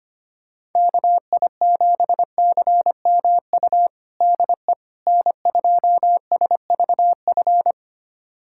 Morse code, heard as a keyed tone, about 25 wpm, 700 Hz. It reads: KI7CMU DE N2SVF
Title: USA Callsign DE Callsign 25wpm